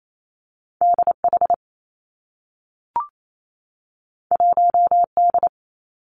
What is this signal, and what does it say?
Morse code, heard as a keyed tone, about 28 wpm, 700 Hz. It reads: D H 1 D
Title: Two Challenging Letter-Number Combinations 28wpm